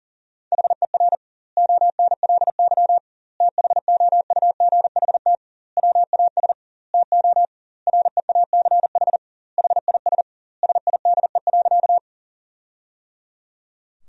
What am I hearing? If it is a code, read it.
Morse code, heard as a keyed tone, about 40 wpm, 700 Hz. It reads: HER ONLY THOUGHT WAS TO REACH HIS SIDE.
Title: 2x Word-Spacing - Sentences from Top 400 Words - 40wpm